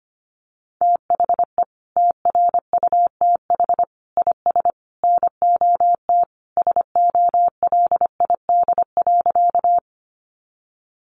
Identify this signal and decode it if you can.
Morse code, heard as a keyed tone, about 25 wpm, 700 Hz. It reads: THE TRUTH IS NOT SOLID.
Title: Sentences from Top 1500 Words 25wpm